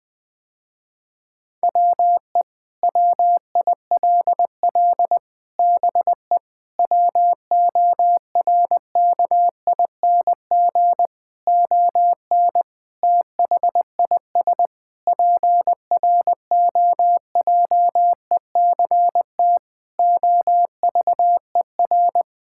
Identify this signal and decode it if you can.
Morse code, heard as a keyed tone, about 20 wpm, 700 Hz. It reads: WE WILL BE WORKING ON THIS PROJECT OVER
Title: Sentences with 6-Word Phrases - Spoken Beginning - 20wpm